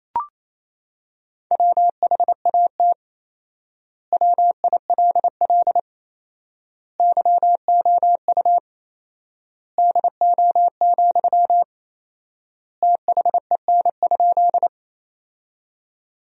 Morse code, heard as a keyed tone, about 28 wpm, 700 Hz. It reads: WHAT WILL YOU DO, THEN?
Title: 4x Word-Spacing - Sentences from Top 100 Words - 28wpm